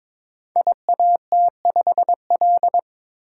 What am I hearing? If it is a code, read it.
Morse code, heard as a keyed tone, about 23 wpm, 700 Hz. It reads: IAT5L
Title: Code Groups - 5 Letter:Numbers - 22wpm